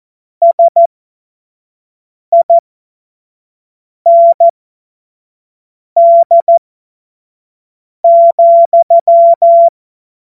Morse code, heard as a keyed tone, about 14 wpm, spaced as for 5 wpm, 680 Hz. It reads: SIND,
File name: Morsekurs / Witze / Witz 17. Januar 2023 25 F70